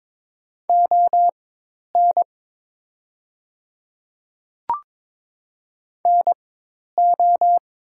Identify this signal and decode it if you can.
Morse code, heard as a keyed tone, about 22 wpm, spaced as for 10 wpm, 700 Hz. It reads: ON NO
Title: 4x Character Spacing - Top 100 Words - 22wpm